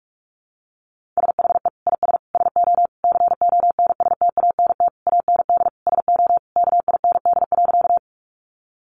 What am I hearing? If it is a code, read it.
Morse code, heard as a keyed tone, about 44 wpm, 700 Hz. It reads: SHE IS SO CONSTANT AND SO KIND.
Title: Sentences from Top 1500 Words 45wpm